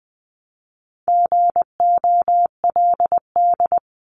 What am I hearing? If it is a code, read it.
Morse code, heard as a keyed tone, about 20 wpm, 700 Hz. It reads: GOLD
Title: Multi-letter Phonemes Repeated 3x 20wpm